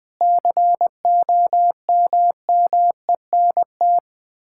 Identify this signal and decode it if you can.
Morse code, heard as a keyed tone, about 20 wpm, 700 Hz. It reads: COMMENT
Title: Sets of 8 Words - 20wpm